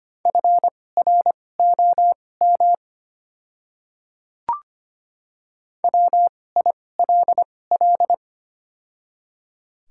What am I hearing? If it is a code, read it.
Morse code, heard as a keyed tone, about 25 wpm, spaced as for 18 wpm, 700 Hz. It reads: FROM WILL
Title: 2x Character Spacing - Top 100 Words - 25wpm